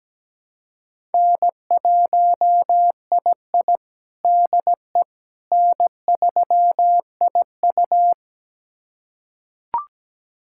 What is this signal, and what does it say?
Morse code, heard as a keyed tone, about 17 wpm, 700 Hz. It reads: N1II DE N3IU
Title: USA Callsign DE Callsign - 4 Characters - 17wpm